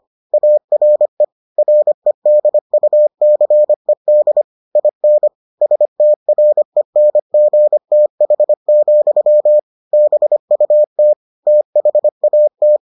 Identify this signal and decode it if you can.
Morse code, heard as a keyed tone, about 25 wpm, 600 Hz. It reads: ARE REDUCED IN STRENGTH, BUT THAT